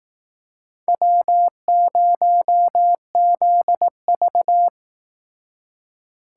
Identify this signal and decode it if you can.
Morse code, heard as a keyed tone, about 18 wpm, 700 Hz. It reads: W0ZV
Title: Speed Racing - USA 1x2 Call Signs 15wpm